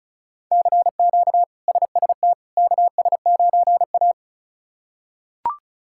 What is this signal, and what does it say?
Morse code, heard as a keyed tone, about 34 wpm, 700 Hz. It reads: CQ SST KS9A
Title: K1USN SST QSO Practice - 35wpm